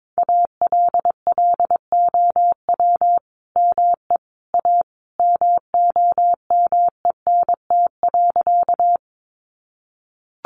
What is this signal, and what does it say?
Morse code, heard as a keyed tone, about 22 wpm, 700 Hz. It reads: ALLOW ME A MOMENT.
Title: Sentences from Top 1000 Words 22wpm